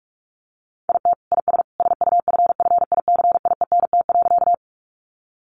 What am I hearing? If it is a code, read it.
Morse code, heard as a keyed tone, about 44 wpm, 700 Hz. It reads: IT IS SUFFICIENT.
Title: Sentences from Top 1400 Words 45wpm